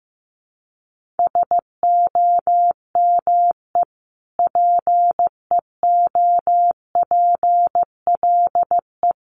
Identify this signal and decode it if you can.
Morse code, heard as a keyed tone, about 15 wpm, 700 Hz. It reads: SOME PEOPLE
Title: Sentences from Top 1500 Words 15wpm